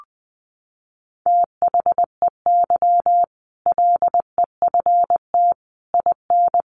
Morse code, heard as a keyed tone, about 19 wpm, 700 Hz. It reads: THEY LEFT IN
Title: Sentences from Top 400 Words 20wpm